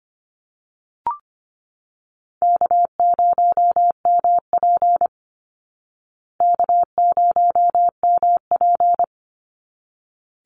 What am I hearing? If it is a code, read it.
Morse code, heard as a keyed tone, about 25 wpm, 700 Hz. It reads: K0MP K0MP
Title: Call Signs Repeated 3x - 25wpm